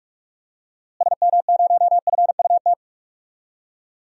Morse code, heard as a keyed tone, about 42 wpm, 700 Hz. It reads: SM0FUT
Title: DXCC-100 Call Signs - 6 Characters - Repeated 2x - 45wpm